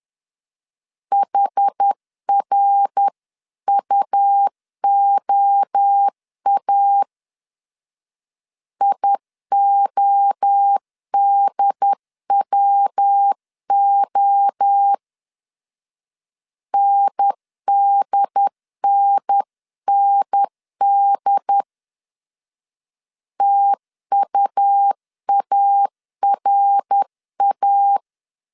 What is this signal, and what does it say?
Morse code, heard as a keyed tone, about 10 wpm, 800 Hz. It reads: HRUOA IODWO NDNND TUARA
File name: 10wpm